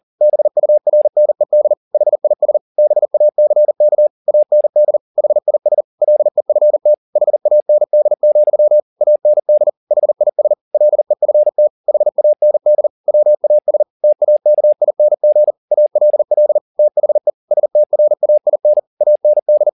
Morse code, heard as a keyed tone, about 40 wpm, 600 Hz. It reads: BURNED HIS BACK AND HIS LEFT HAND, AND HIS LEFT HAND WAS TAKING ALL THE STRAIN AND
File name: the old man and the sea 2 40wpm